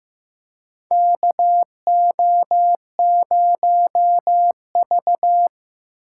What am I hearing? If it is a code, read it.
Morse code, heard as a keyed tone, about 15 wpm, 700 Hz. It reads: KO0V
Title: USA Call Signs - 8x Character Spacing - 15wpm